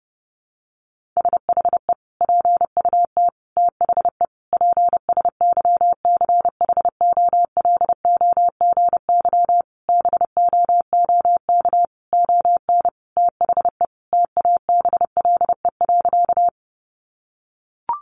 Morse code, heard as a keyed tone, about 30 wpm, 700 Hz. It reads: SHE PUT THE PSYCHOLOGY BOOK ON THE TABLE.
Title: Sentences from Top 1400 Words 30wpm